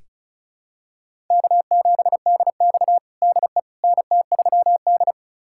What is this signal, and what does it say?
Morse code, heard as a keyed tone, about 34 wpm, 700 Hz. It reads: K7DX DE NT3D
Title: USA Callsign DE Callsign - 4 Characters - 35wpm